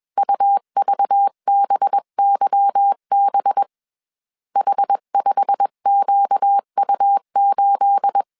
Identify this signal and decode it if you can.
Morse code, heard as a keyed tone, about 21 wpm, 800 Hz. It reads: UVBYB H5QU8
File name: CW / DET / 1 9 1500C S B 20wpm 2313